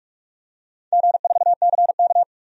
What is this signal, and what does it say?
Morse code, heard as a keyed tone, about 46 wpm, 700 Hz. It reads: G4CK
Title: DXCC-100 Call Signs - 4 Characters - Repeated 3x - 45wpm